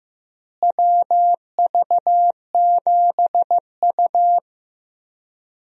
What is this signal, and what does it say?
Morse code, heard as a keyed tone, about 15 wpm, 700 Hz. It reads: WV7U
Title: USA Call Signs - 4 Characters - Repeated 3x - 15wpm